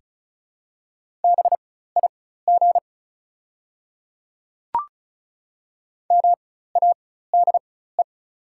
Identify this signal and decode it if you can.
Morse code, heard as a keyed tone, about 36 wpm, spaced as for 16 wpm, 700 Hz. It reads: BIG MADE
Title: 4x Character Spacing - Top 300 Words - 35wpm